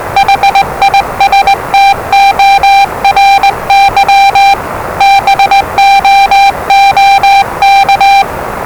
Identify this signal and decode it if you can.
Morse code, heard as a keyed tone, about 20 wpm, 790 Hz, with noise. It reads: HISTORY BOOK
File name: WellingtonRadioZlwCloseDownOnHfAndMf / 01-Zlw 4277khz 417.5 qrt until 0925z